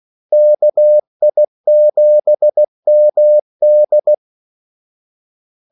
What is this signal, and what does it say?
Morse code, heard as a keyed tone, about 16 wpm, 600 Hz. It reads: KI7MD
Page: Kaldesignaler 16 wpm